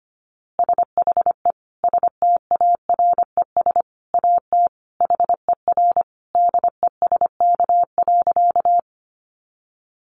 Morse code, heard as a keyed tone, about 25 wpm, 700 Hz. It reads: SHE STARES AT HER DESK.
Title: Easy Phrases 25wpm